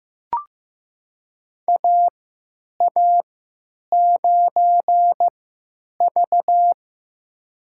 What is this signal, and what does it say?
Morse code, heard as a keyed tone, about 15 wpm, spaced as for 9 wpm, 700 Hz. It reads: AA9V
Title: USA Call Signs - 4 Characters - 3x Character Spacing - 15wpm